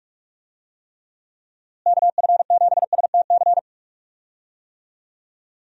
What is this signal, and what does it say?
Morse code, heard as a keyed tone, about 46 wpm, 700 Hz. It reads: KF7STC
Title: USA Call Signs - 6 Characters - Repeated 3x - 45wpm